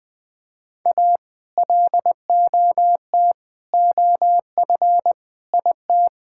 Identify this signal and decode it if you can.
Morse code, heard as a keyed tone, about 20 wpm, 700 Hz. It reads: A LOT OF IT
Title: Sentences with 8-Word Phrases - Spoken Beginning - 20wpm